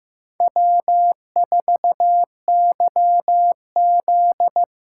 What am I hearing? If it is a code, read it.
Morse code, heard as a keyed tone, about 15 wpm, 700 Hz. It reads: W4YZ
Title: USA Callsign DE Callsign - 4 Characters - 15wpm